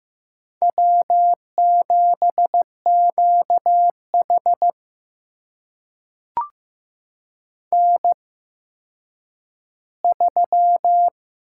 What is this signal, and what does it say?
Morse code, heard as a keyed tone, about 15 wpm, spaced as for 9 wpm, 700 Hz. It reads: W7QH N3
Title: USA Call Signs - 4 Characters - 8x Character Spacing - 15wpm